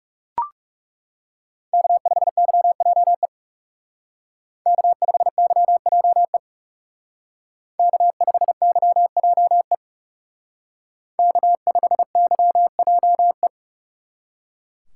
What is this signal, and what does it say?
Morse code, heard as a keyed tone, about 37 wpm, 700 Hz. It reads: K5YJE K5YJE K5YJE K5YJE
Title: Speed Racing - USA 1x3 Call Signs 30wpm